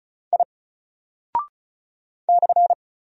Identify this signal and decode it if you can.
Morse code, heard as a keyed tone, about 36 wpm, 700 Hz. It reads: I /
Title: Single Character (Letter:Number:Punctuation:Prosign) Alt 35wpm